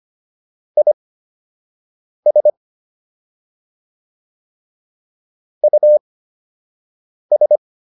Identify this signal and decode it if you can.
Morse code, heard as a keyed tone, about 25 wpm, spaced as for 6 wpm, 600 Hz. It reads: IS US